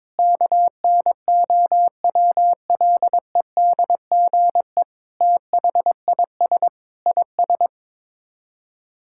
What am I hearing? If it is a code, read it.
Morse code, heard as a keyed tone, about 21 wpm, 700 Hz. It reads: KNOWLEDGE THIS IS
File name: practice / 22wpm / Sets of 8 Words - 22wpm